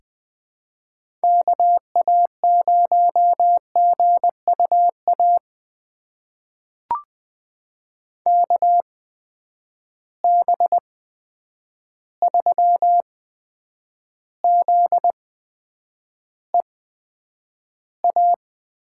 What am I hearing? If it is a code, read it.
Morse code, heard as a keyed tone, about 20 wpm, spaced as for 9 wpm, 700 Hz. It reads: KA0GUA KB3ZEA
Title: USA Call Signs - 6 Characters - 8x Character Spacing - 20wpm